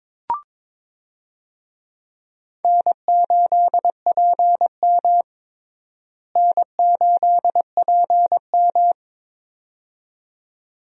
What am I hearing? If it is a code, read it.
Morse code, heard as a keyed tone, about 22 wpm, 700 Hz. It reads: N8PM N8PM
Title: USA Call Signs - 4 Characters - Repeated 2x - 22wpm